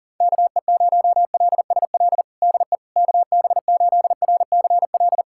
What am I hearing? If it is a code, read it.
Morse code, heard as a keyed tone, about 40 wpm, 700 Hz. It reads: KE0LSL DE KB8RCL
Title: USA Callsign DE Callsign 40wpm